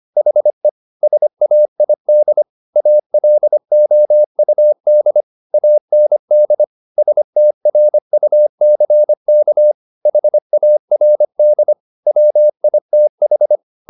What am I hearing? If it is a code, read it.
Morse code, heard as a keyed tone, about 25 wpm, 600 Hz. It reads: HE SAID ALOUD AND STRUCK HARD WITH
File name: the old man and the sea part 1 25wpm